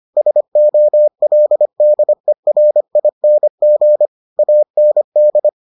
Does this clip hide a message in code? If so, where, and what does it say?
Yes: Morse code from 0.2 to 5.5 s, heard as a keyed tone, about 25 wpm, 600 Hz. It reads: SOLDERING AND